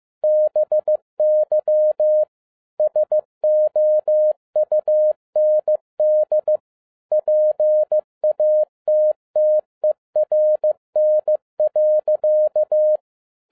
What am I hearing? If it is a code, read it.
Morse code, heard as a keyed tone, about 15 wpm, 610 Hz. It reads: BY SOUND PATTERN.
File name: MY CW LIFE 1 15wpm